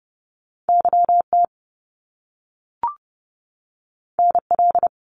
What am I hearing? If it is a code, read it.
Morse code, heard as a keyed tone, about 30 wpm, 700 Hz. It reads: YT NL
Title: Canadian Provinces and Territories 30wpm